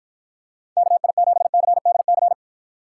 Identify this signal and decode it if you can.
Morse code, heard as a keyed tone, about 54 wpm, 700 Hz. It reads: KI7CDC